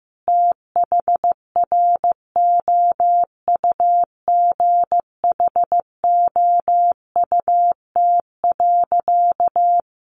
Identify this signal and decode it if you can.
Morse code, heard as a keyed tone, about 15 wpm, 700 Hz. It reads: THROUGHOUT.
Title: Sentences from Top 800 Words 15wpm